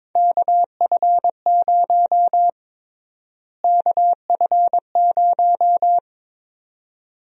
Morse code, heard as a keyed tone, about 22 wpm, 700 Hz. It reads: KF0 KF0
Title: USA Call Sign Prefixes - Repeated 2x - 22wpm